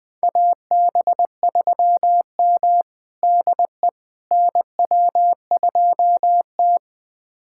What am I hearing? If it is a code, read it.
Morse code, heard as a keyed tone, about 20 wpm, 700 Hz. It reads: AB3M DE NW2T
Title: USA Callsign DE Callsign - 4 Characters - 20wpm